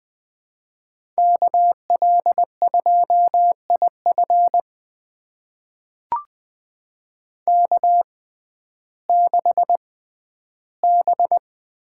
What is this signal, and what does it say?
Morse code, heard as a keyed tone, about 20 wpm, spaced as for 12 wpm, 700 Hz. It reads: KL2IF K6B
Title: USA Call Signs - 5 Characters - 6x Character Spacing - 20wpm